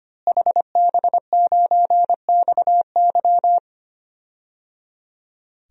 Morse code, heard as a keyed tone, about 25 wpm, 700 Hz. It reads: HB9XY
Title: DXCC-100 Call Signs - 5 Characters - Repeated 3x - 25wpm